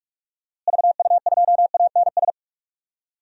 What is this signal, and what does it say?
Morse code, heard as a keyed tone, about 42 wpm, 700 Hz. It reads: VU2ANS
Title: DXCC-100 Call Signs - 6 Characters - Repeated 3x - 45wpm